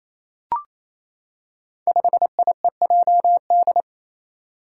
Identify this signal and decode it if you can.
Morse code, heard as a keyed tone, about 28 wpm, 700 Hz. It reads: 5IEJD
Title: Five Challenging Letter-Number Combinations 28wpm